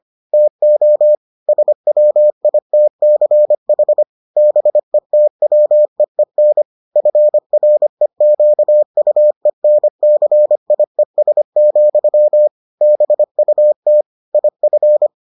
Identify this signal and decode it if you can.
Morse code, heard as a keyed tone, about 25 wpm, 600 Hz. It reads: TO SWITCH BETWEEN FREQUENCIES, BUT IF